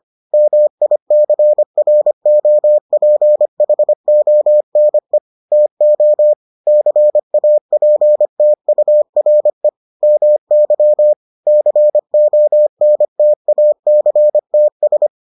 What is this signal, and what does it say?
Morse code, heard as a keyed tone, about 25 wpm, 600 Hz. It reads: MICROPHONE TO CAPTURE MY CONTACTS